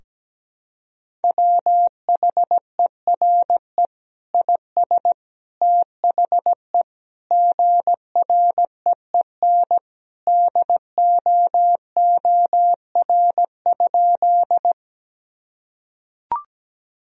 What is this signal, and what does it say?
Morse code, heard as a keyed tone, about 17 wpm, 700 Hz. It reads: WHERE IS THE GREEN DOOR?
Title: Sentences from Top 1000 Words - Review Part 2 - 17wpm